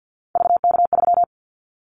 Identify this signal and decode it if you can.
Morse code, heard as a keyed tone, about 50 wpm, 700 Hz. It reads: VK3
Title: Callsign Prefixes for Top 100 DXCC - 50wpm